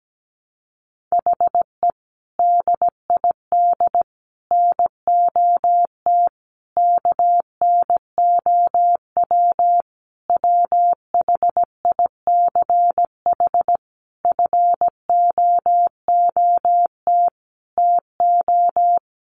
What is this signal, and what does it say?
Morse code, heard as a keyed tone, about 17 wpm, 700 Hz. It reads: HE DID NOT KNOW WHICH FOOT TO
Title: Sentences from Top 1300 Words 17wpm